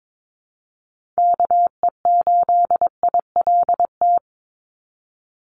Morse code, heard as a keyed tone, about 22 wpm, 700 Hz. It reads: KE8ILT
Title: USA Call Signs 22wpm